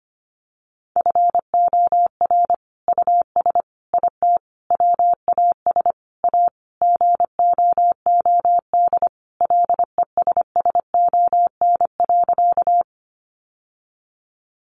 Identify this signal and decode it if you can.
Morse code, heard as a keyed tone, about 25 wpm, 700 Hz. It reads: FOR US IT WAS A GOOD LESSON.
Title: Sentences from Top 1200 Words 25wpm